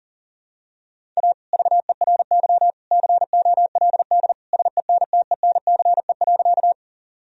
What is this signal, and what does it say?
Morse code, heard as a keyed tone, about 37 wpm, 700 Hz. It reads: A VERY COLD SENTENCE.
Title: Sentences from Top 1000 Words - Review Part 1 - 40wpm